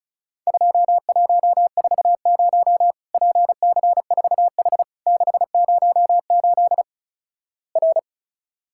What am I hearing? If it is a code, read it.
Morse code, heard as a keyed tone, about 36 wpm, 700 Hz. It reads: 2140 PC4H 608 R
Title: Worked All Europe DX Contest - QTC Exchange - 35wpm